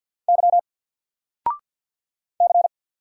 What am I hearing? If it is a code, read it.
Morse code, heard as a keyed tone, about 50 wpm, 700 Hz. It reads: Y C
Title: Single Character (Letter:Number:Punctuation:Prosign) 50wpm